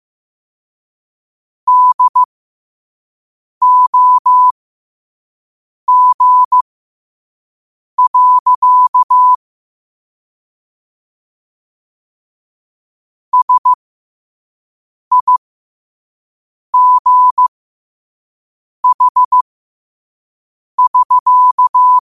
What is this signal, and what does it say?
Morse code, heard as a keyed tone, about 15 wpm, spaced as for 5 wpm, 1000 Hz. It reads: DOG. SIGH<SK>
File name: qotdfiles / 5wpm